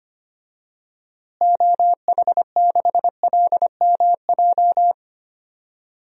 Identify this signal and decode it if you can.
Morse code, heard as a keyed tone, about 25 wpm, 700 Hz. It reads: OH6LMJ
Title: DXCC-100 Call Signs - 6 Characters - Repeated 3x - 25wpm